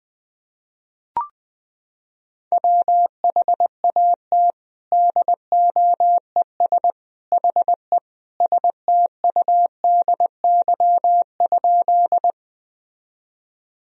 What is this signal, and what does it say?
Morse code, heard as a keyed tone, about 20 wpm, 700 Hz. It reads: WHAT DOES HE STUDY?
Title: Sentences from Top 1000 Words - Review Part 1 - 20wpm